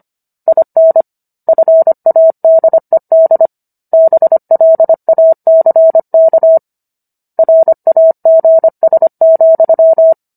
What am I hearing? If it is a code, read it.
Morse code, heard as a keyed tone, about 25 wpm, 650 Hz. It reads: IN FADED BLACK RAGS,